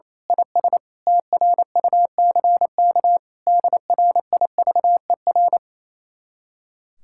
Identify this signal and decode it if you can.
Morse code, heard as a keyed tone, about 26 wpm, 700 Hz. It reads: IS TRUCK DRIVER
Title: Speed-Racing Common QSO Phrases Encore - 28wpm